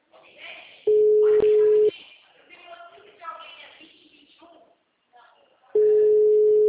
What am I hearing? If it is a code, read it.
Ringback tone (Europe, 425 Hz), with speech.
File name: out- -2007- - - .56948